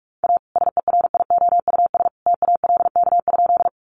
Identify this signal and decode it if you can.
Morse code, heard as a keyed tone, about 44 wpm, 700 Hz. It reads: A SERIOUS TALK?
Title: Sentences from Top 1100 Words 45wpm